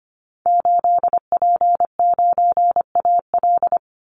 Morse code, heard as a keyed tone, about 25 wpm, 700 Hz. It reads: 8P9AL
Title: Call Signs Repeated 3x - 25wpm